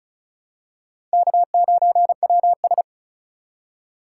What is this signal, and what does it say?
Morse code, heard as a keyed tone, about 34 wpm, 700 Hz. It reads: K9WS
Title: USA Call Signs - 4 Characters - 4x Character Spacing - 35wpm